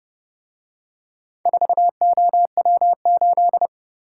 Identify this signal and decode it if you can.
Morse code, heard as a keyed tone, about 31 wpm, 700 Hz. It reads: 4OW8